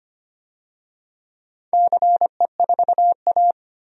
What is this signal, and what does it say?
Morse code, heard as a keyed tone, about 25 wpm, 700 Hz. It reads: CE4A
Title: DXCC-100 Call Signs - 4 Characters - 25wpm